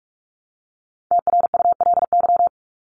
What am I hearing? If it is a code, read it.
Morse code, heard as a keyed tone, about 44 wpm, 700 Hz. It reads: TRULY
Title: Top 1501-1750 Words 45wpm